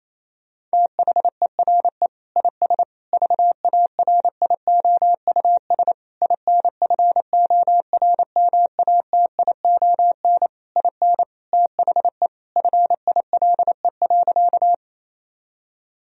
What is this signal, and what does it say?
Morse code, heard as a keyed tone, about 27 wpm, 700 Hz. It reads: THERE IS VARIOUS INFORMATION IN THE FILE.
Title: Sentences from Top 1000 Words - Review Part 4 - 28wpm